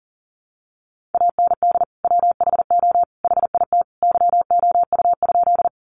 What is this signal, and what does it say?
Morse code, heard as a keyed tone, about 38 wpm, 700 Hz. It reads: AND WHO HIT YOU?
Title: Sentences from Top 1200 Words 40wpm